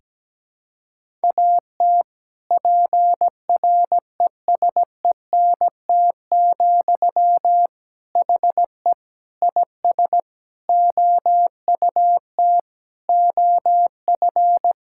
Morse code, heard as a keyed tone, about 17 wpm, 700 Hz. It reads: AT PRESENT, HE IS OUT OF
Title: Sentences from Top 1000 Words - Review Part 5 - 17wpm